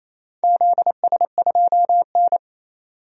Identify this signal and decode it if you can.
Morse code, heard as a keyed tone, about 29 wpm, 700 Hz. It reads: ZS2N
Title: DXCC-100 Call Signs - 4 Characters - 28wpm